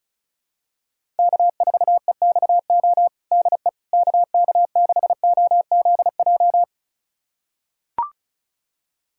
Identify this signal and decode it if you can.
Morse code, heard as a keyed tone, about 34 wpm, 700 Hz. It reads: K4EXO DE KK6OZJ
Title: USA Callsign DE Callsign 35wpm